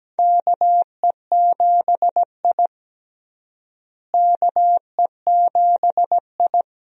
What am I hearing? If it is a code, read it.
Morse code, heard as a keyed tone, about 17 wpm, 700 Hz. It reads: KE7I KE7I
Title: USA Call Signs - 4 Characters - Repeated 2x - 17wpm